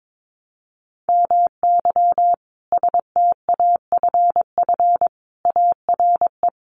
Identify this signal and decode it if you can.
Morse code, heard as a keyed tone, about 22 wpm, 700 Hz. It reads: MY STAFF ARE
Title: Sentences from Top 1200 Words 22wpm